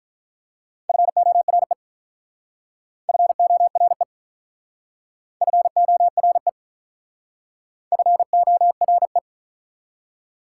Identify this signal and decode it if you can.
Morse code, heard as a keyed tone, about 47 wpm, 700 Hz. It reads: FORE FORE FORE FORE